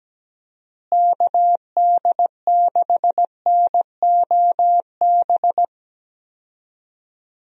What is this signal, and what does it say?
Morse code, heard as a keyed tone, about 17 wpm, 700 Hz. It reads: KD6NOB
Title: USA Call Signs - 6 Characters - Repeated 3x - 17wpm